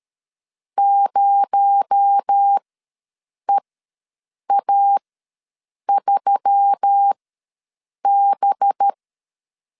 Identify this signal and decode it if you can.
Morse code, heard as a keyed tone, about 13 wpm, spaced as for 7 wpm, 800 Hz. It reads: 0EA3B
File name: LEZIONI / 6 PKX0 / 6 PKX0 ws 12wpm 3448